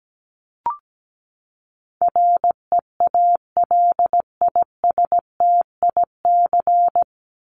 Morse code, heard as a keyed tone, about 17 wpm, 700 Hz. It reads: REALISTIC
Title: Top 1751-2000 Words 17wpm